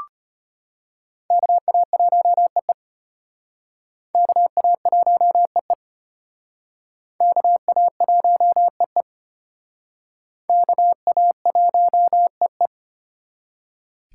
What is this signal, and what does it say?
Morse code, heard as a keyed tone, about 32 wpm, 700 Hz. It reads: KA1EE KA1EE KA1EE KA1EE